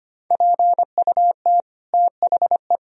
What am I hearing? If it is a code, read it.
Morse code, heard as a keyed tone, about 25 wpm, 700 Hz. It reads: PUT THE
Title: Sentences from Top 1000 Words - Review Part 3 - 25wpm